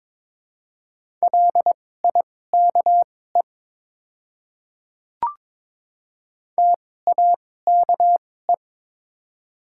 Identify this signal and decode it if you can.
Morse code, heard as a keyed tone, about 22 wpm, spaced as for 16 wpm, 700 Hz. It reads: LIKE TAKE
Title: 2x Character Spacing - Top 200 Words - 22wpm